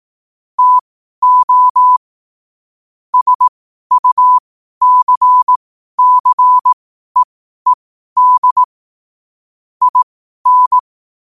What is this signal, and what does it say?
Morse code, heard as a keyed tone, about 18 wpm, spaced as for 12 wpm, 1000 Hz. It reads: TO SUCCEED IN
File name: qotdfiles / 12wpm